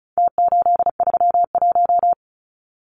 Morse code, heard as a keyed tone, about 34 wpm, 700 Hz. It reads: T831
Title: Numbers - Contest Style - 4-Digit 35wpm